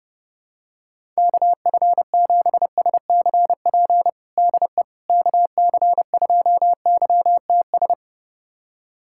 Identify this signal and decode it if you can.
Morse code, heard as a keyed tone, about 30 wpm, 700 Hz. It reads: KF7SCP DE KC2YTS
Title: USA Callsign DE Callsign - 6 Characters - 30wpm